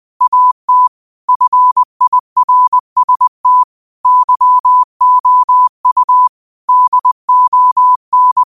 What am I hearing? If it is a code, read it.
Morse code, heard as a keyed tone, about 20 wpm, 1000 Hz. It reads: AT FIRST YOU DON
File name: qotdfiles / 20wpm